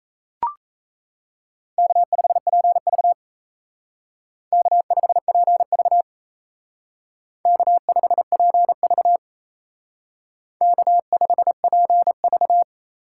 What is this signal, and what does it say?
Morse code, heard as a keyed tone, about 33 wpm, 700 Hz. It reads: K5PV K5PV K5PV K5PV
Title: Speed Racing - USA 1x2 Call Signs 28wpm